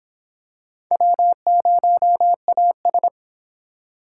Morse code, heard as a keyed tone, about 25 wpm, 700 Hz. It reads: W0AS